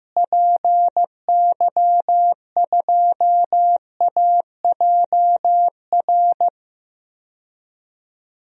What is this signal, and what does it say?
Morse code, heard as a keyed tone, about 15 wpm, 700 Hz. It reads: PY2AJR
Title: DXCC-100 Call Signs - 6 Characters - Repeated 3x - 15wpm